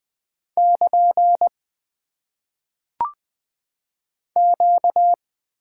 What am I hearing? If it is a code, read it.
Morse code, heard as a keyed tone, about 20 wpm, 700 Hz. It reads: <KN> Q
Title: Top 90 QSO Elements - 8x Character Spacing - 20wpm